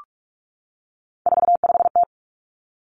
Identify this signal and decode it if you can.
Morse code, heard as a keyed tone, about 46 wpm, 700 Hz. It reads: 45T
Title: Three Challenging Letter-Number Combinations 45wpm